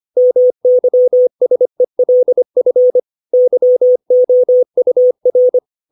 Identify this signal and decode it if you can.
Morse code, heard as a keyed tone, about 25 wpm, 500 Hz. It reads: MYSELF YOUR
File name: 632 words / 632 Words 25WPM